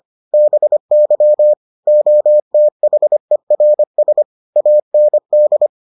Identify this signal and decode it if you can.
Morse code, heard as a keyed tone, about 25 wpm, 600 Hz. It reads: BY OTHERS AND